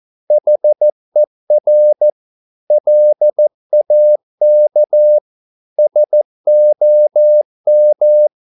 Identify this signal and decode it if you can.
Morse code, heard as a keyed tone, about 14 wpm, 600 Hz. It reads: HER LAK SOM
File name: Korte ord DK 14wpm